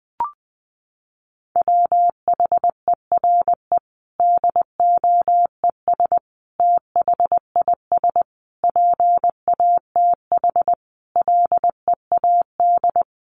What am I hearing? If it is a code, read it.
Morse code, heard as a keyed tone, about 19 wpm, 700 Hz. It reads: WHERE DOES THIS PATH LEAD
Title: Sentences from Top 1400 Words 20wpm